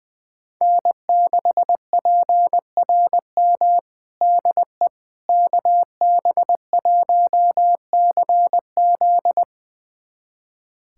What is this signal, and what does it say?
Morse code, heard as a keyed tone, about 20 wpm, 700 Hz. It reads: N6PRM DE KB1CZ
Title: USA Callsign DE Callsign - 5 Characters - 20wpm